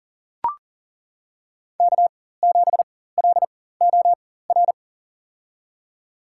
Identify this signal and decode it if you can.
Morse code, heard as a keyed tone, about 40 wpm, spaced as for 19 wpm, 700 Hz. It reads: K7LOR
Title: USA Call Signs - 4x Character Spacing - 40wpm